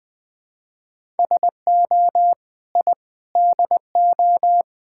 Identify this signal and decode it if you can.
Morse code, heard as a keyed tone, about 20 wpm, 700 Hz. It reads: SO I DO
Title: Build A Phrase - 5-Word Phrases - 20wpm